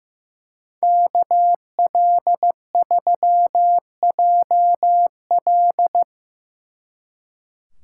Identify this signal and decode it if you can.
Morse code, heard as a keyed tone, about 18 wpm, 700 Hz. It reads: KL3JL
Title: Speed Racing - USA 2x2 Call Signs 15wpm